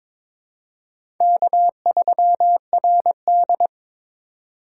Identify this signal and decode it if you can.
Morse code, heard as a keyed tone, about 23 wpm, 700 Hz. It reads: K3RD
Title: USA Call Signs - 4 Characters - 8x Character Spacing - 22wpm